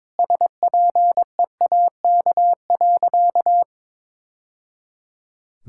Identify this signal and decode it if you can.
Morse code, heard as a keyed tone, about 22 wpm, 700 Hz. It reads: SPEAK.
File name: practice / 22wpm / Sentences with 2-Word Phrases - 22wpm